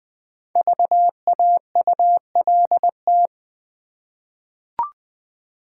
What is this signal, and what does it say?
Morse code, heard as a keyed tone, about 20 wpm, 690 Hz. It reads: VAULT
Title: Easy Words with Hints - 20wpm